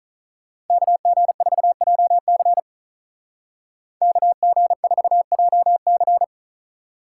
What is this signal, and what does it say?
Morse code, heard as a keyed tone, about 47 wpm, 700 Hz. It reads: KG4JC KG4JC